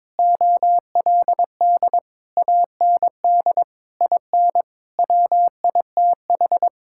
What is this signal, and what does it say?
Morse code, heard as a keyed tone, about 21 wpm, 700 Hz. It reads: OLD AND IN WITH
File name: Sentences with 9-Word Phrases - 22wpm